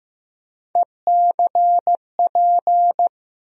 Morse code, heard as a keyed tone, about 15 wpm, 700 Hz. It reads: ECP
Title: USA Call Sign Postfixes - 3 Letters 15wpm